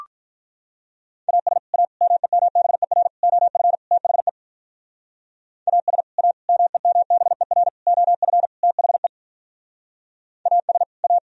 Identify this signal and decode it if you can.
Morse code, heard as a keyed tone, about 46 wpm, 700 Hz. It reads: AS A MEMBER OF THE AS A MEMBER OF THE AS A